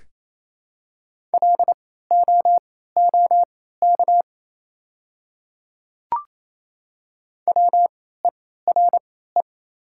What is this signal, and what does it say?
Morse code, heard as a keyed tone, about 27 wpm, spaced as for 16 wpm, 700 Hz. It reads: LOOK WERE
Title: 3x Character Spacing - Top 200 Words - 28wpm